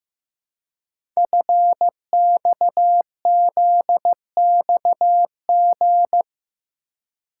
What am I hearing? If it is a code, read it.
Morse code, heard as a keyed tone, about 15 wpm, 700 Hz. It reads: FXZXG
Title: Code Groups - 5 Letters - 15wpm